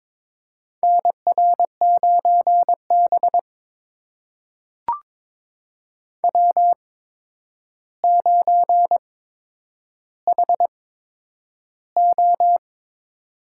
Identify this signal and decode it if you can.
Morse code, heard as a keyed tone, about 22 wpm, spaced as for 9 wpm, 700 Hz. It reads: NR9B W9HO
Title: USA Call Signs - 4 Characters - 8x Character Spacing - 22wpm